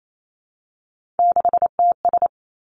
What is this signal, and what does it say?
Morse code, heard as a keyed tone, about 28 wpm, 700 Hz. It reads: 6TS
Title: Three Challenging Letter-Number Combinations 28wpm